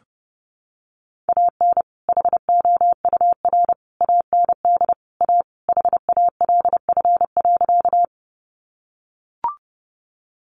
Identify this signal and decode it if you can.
Morse code, heard as a keyed tone, about 30 wpm, 700 Hz. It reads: AN HOUR AND A HALF.
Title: Sentences from Top 400 Words 30wpm